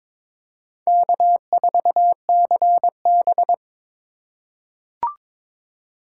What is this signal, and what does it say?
Morse code, heard as a keyed tone, about 22 wpm, 700 Hz. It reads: K4CB
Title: USA Call Signs - 4 Characters - 3x Character Spacing - 22wpm